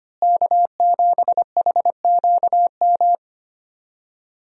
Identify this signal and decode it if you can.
Morse code, heard as a keyed tone, about 25 wpm, 700 Hz. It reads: K7HQM
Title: USA Call Signs - 5 Characters - Repeated 2x - 25wpm